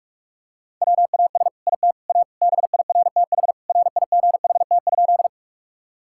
Morse code, heard as a keyed tone, about 42 wpm, 700 Hz. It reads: WAS IT A BIRTH RIGHT?
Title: Sentences from Top 1000 Words - Review Part 5 - 45wpm